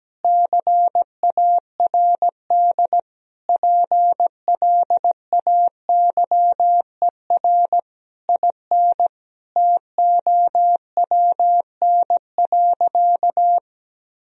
Morse code, heard as a keyed tone, about 17 wpm, 700 Hz. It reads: CARD PLAYER IN TOWN.
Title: Sentences from Top 1000 Words - Review Part 4 - 17wpm